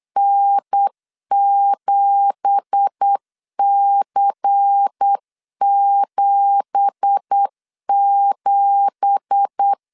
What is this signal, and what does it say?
Morse code, heard as a keyed tone, about 8 wpm, 800 Hz. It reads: N7C77
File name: 8wpm